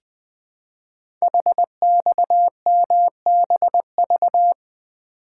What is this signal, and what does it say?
Morse code, heard as a keyed tone, about 20 wpm, 700 Hz. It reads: HXMBV
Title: Code Groups - 5 Letters - 20wpm